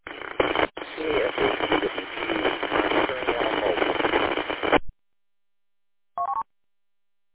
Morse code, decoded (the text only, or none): none